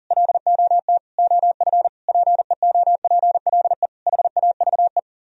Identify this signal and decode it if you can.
Morse code, heard as a keyed tone, about 52 wpm, 700 Hz. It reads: LOT OF PEOPLE HAVE